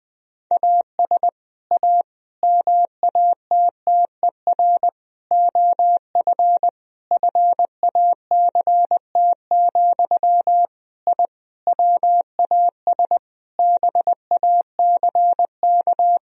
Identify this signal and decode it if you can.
Morse code, heard as a keyed tone, about 20 wpm, 700 Hz. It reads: AS A MATTER OF FACT, I WAS BACK
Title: Sentences from Top 1000 Words - Review Part 4 - 20wpm